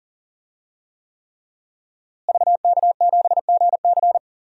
Morse code, heard as a keyed tone, about 40 wpm, 700 Hz. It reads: VK7GC
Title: DXCC-100 Call Signs - 5 Characters - Repeated 3x - 40wpm